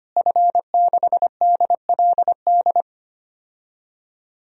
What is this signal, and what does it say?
Morse code, heard as a keyed tone, about 25 wpm, 700 Hz. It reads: F6DLD
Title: DXCC-100 Call Signs - 5 Characters - Repeated 3x - 25wpm